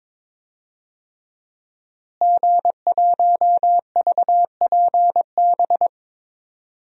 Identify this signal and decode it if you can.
Morse code, heard as a keyed tone, about 22 wpm, 700 Hz. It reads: G1VPB
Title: DXCC-100 Call Signs - 5 Characters - Repeated 3x - 22wpm